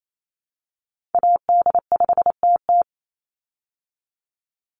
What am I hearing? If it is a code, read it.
Morse code, heard as a keyed tone, about 28 wpm, 700 Hz. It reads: AD5TT
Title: Call Signs Repeated 3x - 28wpm